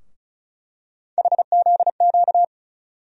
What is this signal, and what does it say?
Morse code, heard as a keyed tone, about 36 wpm, 700 Hz. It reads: HZQ
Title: USA Call Sign Postfixes - 3 Letters 35wpm